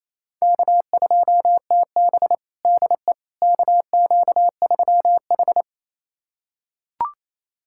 Morse code, heard as a keyed tone, about 28 wpm, 700 Hz. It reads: K2TB DE KQ3H
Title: USA Callsign DE Callsign - 4 Characters - 28wpm